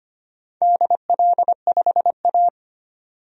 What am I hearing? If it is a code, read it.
Morse code, heard as a keyed tone, about 25 wpm, 700 Hz. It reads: DL5A
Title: DXCC-100 Call Signs - 4 Characters - 25wpm